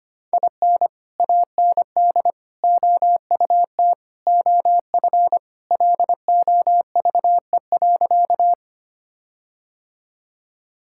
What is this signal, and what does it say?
Morse code, heard as a keyed tone, about 25 wpm, 700 Hz. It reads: IN AND OUT OF LOVE.
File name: practice / 25wpm / Sentences with 4-Word Phrases - 25wpm